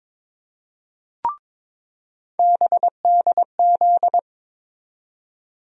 Morse code, heard as a keyed tone, about 22 wpm, 700 Hz. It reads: BDZ
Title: USA Call Sign Postfixes - 3 Letters 22wpm